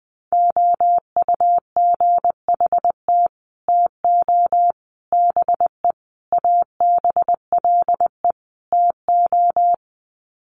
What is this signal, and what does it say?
Morse code, heard as a keyed tone, about 20 wpm, 700 Hz. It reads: OUGHT TO BE ABLE TO
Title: Sets of 5 Words Encore2 20wpm